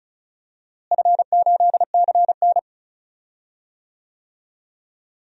Morse code, heard as a keyed tone, about 34 wpm, 700 Hz. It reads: F8CN
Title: DXCC-100 Call Signs - 4 Characters - Repeated 3x - 35wpm